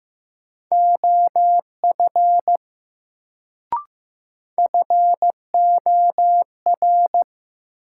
Morse code, heard as a keyed tone, about 15 wpm, 700 Hz. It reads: OF FOR
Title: Sets of 2 Words - New Word Drill - 15wpm